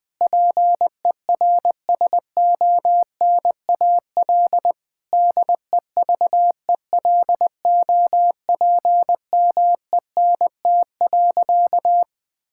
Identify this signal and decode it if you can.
Morse code, heard as a keyed tone, about 20 wpm, 700 Hz. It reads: PERSONAL DEVELOPMENT.
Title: Difficult Sentences with Hints - 20wpm